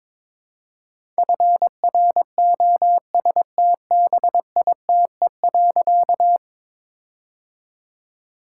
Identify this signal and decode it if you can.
Morse code, heard as a keyed tone, about 22 wpm, 700 Hz. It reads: FROSTBITE.
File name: All-Age Appropriate Jokes - 22wpm